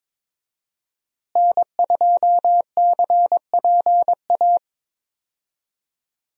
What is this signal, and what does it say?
Morse code, heard as a keyed tone, about 22 wpm, 700 Hz. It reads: N2CPA
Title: USA Call Signs - 5 Characters - Repeated 3x - 22wpm